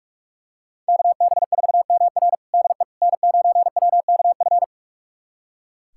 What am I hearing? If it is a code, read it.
Morse code, heard as a keyed tone, about 42 wpm, 700 Hz. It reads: KB4MR DE N9WKF
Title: USA Callsign DE Callsign - 5 Characters - 45wpm